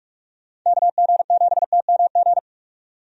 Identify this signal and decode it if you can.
Morse code, heard as a keyed tone, about 46 wpm, 700 Hz. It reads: KG7TMG
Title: USA Call Signs - 6 Characters - 45wpm